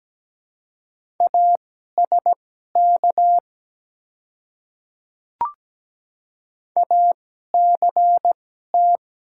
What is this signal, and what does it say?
Morse code, heard as a keyed tone, about 17 wpm, spaced as for 12 wpm, 700 Hz. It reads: ASK ACT
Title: 2x Character Spacing - Top 200 Words - 17wpm